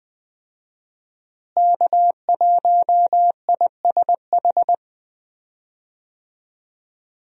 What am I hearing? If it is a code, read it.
Morse code, heard as a keyed tone, about 20 wpm, 700 Hz. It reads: K1ISH
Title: USA Call Signs - 5 Characters - Repeated 3x - 20wpm